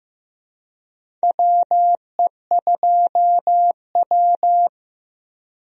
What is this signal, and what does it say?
Morse code, heard as a keyed tone, about 15 wpm, 700 Hz. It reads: WE2W
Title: USA Call Signs - 4 Characters - Repeated 2x - 15wpm